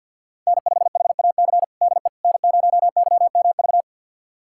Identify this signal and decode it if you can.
Morse code, heard as a keyed tone, about 48 wpm, 700 Hz. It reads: N5HAC DE N0YMV
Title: USA Callsign DE Callsign - 5 Characters - 50wpm